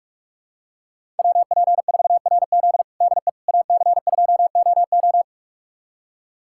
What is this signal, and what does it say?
Morse code, heard as a keyed tone, about 46 wpm, 700 Hz. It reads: WP4RZ DE AC2OO
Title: USA Callsign DE Callsign - 5 Characters - 45wpm